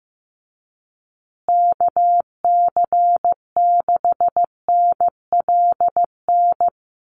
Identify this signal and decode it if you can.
Morse code, heard as a keyed tone, about 15 wpm, 700 Hz. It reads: KC6NLN
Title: USA Call Signs Repeated 3x 15wpm